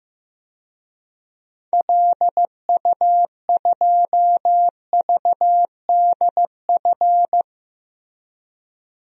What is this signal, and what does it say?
Morse code, heard as a keyed tone, about 15 wpm, 700 Hz. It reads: LU2VDF
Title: DXCC-100 Call Signs - 6 Characters - Repeated 3x - 15wpm